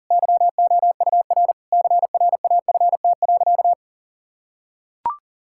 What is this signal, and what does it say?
Morse code, heard as a keyed tone, about 40 wpm, 700 Hz. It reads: YOUR CRAFT.
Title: Sentences from Top 1000 Words - Review Part 6 - 40wpm